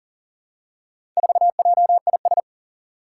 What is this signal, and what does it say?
Morse code, heard as a keyed tone, about 39 wpm, 700 Hz. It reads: 4JIS
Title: Four Challenging Letter-Number Combinations 40wpm